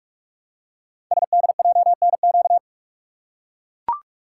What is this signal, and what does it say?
Morse code, heard as a keyed tone, about 44 wpm, 700 Hz. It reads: SDJNQ
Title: Code Groups - 5 Letter:Numbers - 45wpm